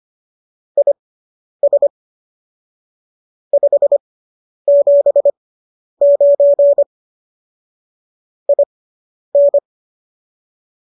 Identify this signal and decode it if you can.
Morse code, heard as a keyed tone, about 25 wpm, spaced as for 10 wpm, 570 Hz. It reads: IS 579 IN